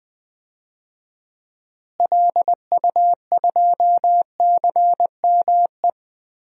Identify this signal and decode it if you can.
Morse code, heard as a keyed tone, about 20 wpm, 700 Hz. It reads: LU2CME
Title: DXCC-100 Call Signs - 6 Characters - Repeated 3x - 20wpm